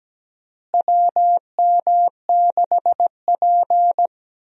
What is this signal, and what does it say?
Morse code, heard as a keyed tone, about 17 wpm, 700 Hz. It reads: WM6P
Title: USA Call Signs - 4 Characters - Repeated 3x - 17wpm